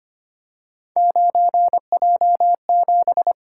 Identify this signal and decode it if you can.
Morse code, heard as a keyed tone, about 25 wpm, 700 Hz. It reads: 9J7
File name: Three Letter-Number Combinations - 25wpm